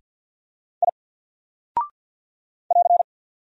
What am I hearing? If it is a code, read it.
Morse code, heard as a keyed tone, about 50 wpm, 700 Hz. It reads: I <AR>
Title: Single Character (Letter:Number:Punctuation:Prosign) 50wpm